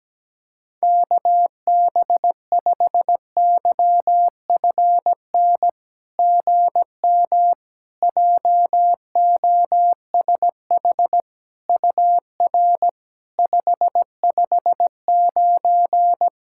Morse code, heard as a keyed tone, about 17 wpm, 700 Hz. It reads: KB5YFN GM JOSH UR 559
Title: SOTA QSO Practice 17wpm